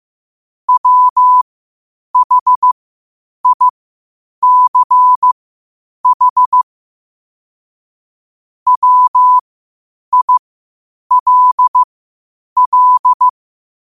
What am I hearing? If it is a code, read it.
Morse code, heard as a keyed tone, about 15 wpm, spaced as for 8 wpm, 1000 Hz. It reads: WHICH WILL
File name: qotdfiles / 8wpm